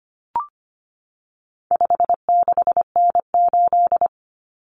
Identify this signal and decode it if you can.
Morse code, heard as a keyed tone, about 25 wpm, 700 Hz. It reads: 56N8
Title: Common Cut Numbers - 4-Digit 25wpm